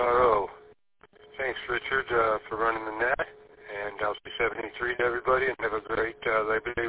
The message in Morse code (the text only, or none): none